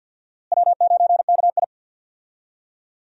Morse code, heard as a keyed tone, about 47 wpm, 700 Hz. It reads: W9KI
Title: USA Call Signs - 4 Characters - Repeated 3x - 50wpm